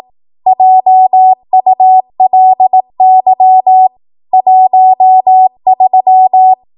Morse code, heard as a keyed tone, about 18 wpm, 750 Hz. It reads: JULY 13
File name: Morse / 18WPM